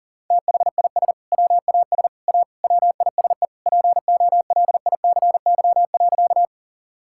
Morse code, heard as a keyed tone, about 40 wpm, 700 Hz. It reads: THIS WAS A WISE POLICY.
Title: Sentences from Top 1600 Words 40wpm